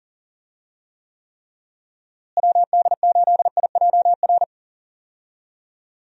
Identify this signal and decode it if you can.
Morse code, heard as a keyed tone, about 40 wpm, 700 Hz. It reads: WD8IJR
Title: USA Call Signs - 6 Characters - Repeated 3x - 40wpm